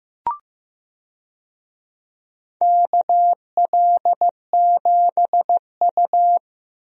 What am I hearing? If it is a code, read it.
Morse code, heard as a keyed tone, about 15 wpm, 700 Hz. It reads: KL7U
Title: DXCC-100 Call Signs - 4 Characters - Repeated 2x - 15wpm